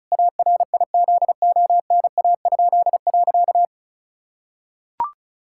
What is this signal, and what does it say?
Morse code, heard as a keyed tone, about 34 wpm, 700 Hz. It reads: ARIZONA?.
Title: Pangram Challenge v2 - 35wpm